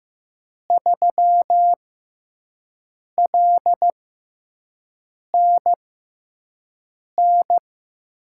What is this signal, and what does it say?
Morse code, heard as a keyed tone, about 15 wpm, spaced as for 5 wpm, 700 Hz. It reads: 3LNN
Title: USA Call Signs - 6 Characters - 6x Character Spacing - 15wpm